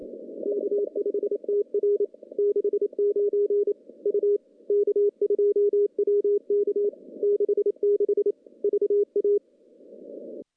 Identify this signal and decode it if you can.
Morse code, heard as a keyed tone, about 28 wpm, 410 Hz. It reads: K5TR 69 U K2WK 66 VA
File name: contests / CW-SS-k5tr- -1300